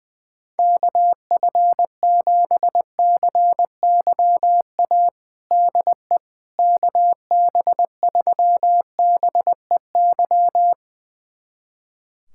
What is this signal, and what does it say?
Morse code, heard as a keyed tone, about 20 wpm, 700 Hz. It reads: KF7CYA DE KB3BEY
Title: USA Callsign DE Callsign - 6 Characters - 20wpm